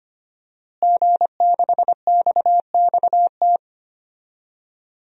Morse code, heard as a keyed tone, about 25 wpm, 700 Hz. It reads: G6XXT
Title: DXCC-100 Call Signs - 5 Characters - Repeated 2x - 25wpm